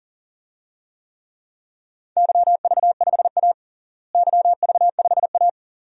Morse code, heard as a keyed tone, about 40 wpm, 700 Hz. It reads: YV5A YV5A
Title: DXCC-100 Call Signs - 4 Characters - Repeated 2x - 40wpm